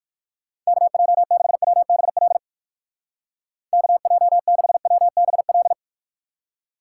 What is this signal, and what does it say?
Morse code, heard as a keyed tone, about 51 wpm, 700 Hz. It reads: KJ6WBL KJ6WBL
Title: Speed Racing - USA Call Signs 45wpm